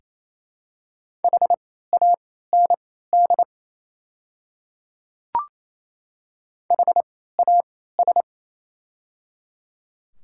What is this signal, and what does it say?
Morse code, heard as a keyed tone, about 28 wpm, spaced as for 16 wpm, 700 Hz. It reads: HAND HAS
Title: 3x Character Spacing - Top 300 Words - 28wpm